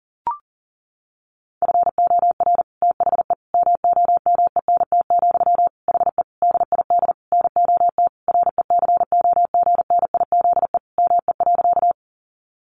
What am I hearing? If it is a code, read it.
Morse code, heard as a keyed tone, about 37 wpm, 700 Hz. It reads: FOR THE MOMENT, HE DID NOT RECOGNIZE ME.
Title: Sentences from Top 1400 Words 40wpm